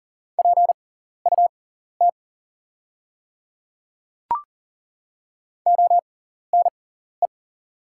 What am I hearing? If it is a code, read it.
Morse code, heard as a keyed tone, about 40 wpm, spaced as for 14 wpm, 690 Hz. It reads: PUT ONE
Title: 6x Character Spacing - Top 200 Words - 40wpm